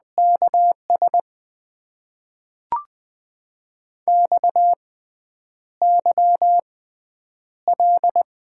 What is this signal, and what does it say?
Morse code, heard as a keyed tone, about 20 wpm, spaced as for 9 wpm, 700 Hz. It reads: KS XYL
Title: Top 90 QSO Elements - 6x Character Spacing - 20wpm